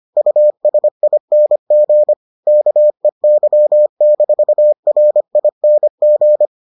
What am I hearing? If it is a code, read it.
Morse code, heard as a keyed tone, about 25 wpm, 600 Hz. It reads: USING KEY-RING